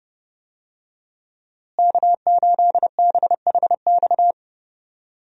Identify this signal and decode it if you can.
Morse code, heard as a keyed tone, about 29 wpm, 700 Hz. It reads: K8BHX
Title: USA Call Signs - 5 Characters - Repeated 3x - 30wpm